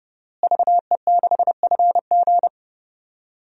Morse code, heard as a keyed tone, about 30 wpm, 700 Hz. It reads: VE6FG
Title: DXCC-100 Call Signs - 5 Characters - Repeated 3x - 30wpm